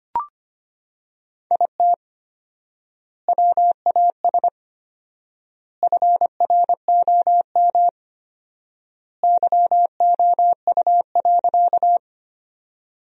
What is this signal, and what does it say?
Morse code, heard as a keyed tone, about 25 wpm, 700 Hz. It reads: IT WAS FROM YOU.
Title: 4x Word-Spacing - Sentences from Top 100 Words - 25wpm